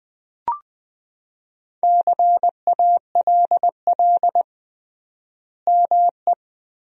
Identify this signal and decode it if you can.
Morse code, heard as a keyed tone, about 20 wpm, 700 Hz. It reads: CALL ME
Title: 3x Word-Spacing - Sentences from Top 200 Words - 20wpm